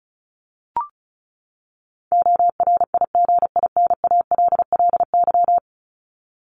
Morse code, heard as a keyed tone, about 34 wpm, 700 Hz. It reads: ORIGINALLY
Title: Top 1251-1500 Words 35wpm